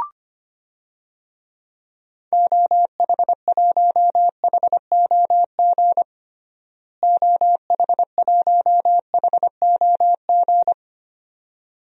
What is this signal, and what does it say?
Morse code, heard as a keyed tone, about 25 wpm, 700 Hz. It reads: OH1HOG OH1HOG
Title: DXCC-100 Call Signs - 6 Characters - Repeated 2x - 25wpm